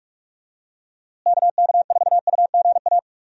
Morse code, heard as a keyed tone, about 48 wpm, 700 Hz. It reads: KK4UGA
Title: USA Call Signs - 6 Characters - Repeated 3x - 45wpm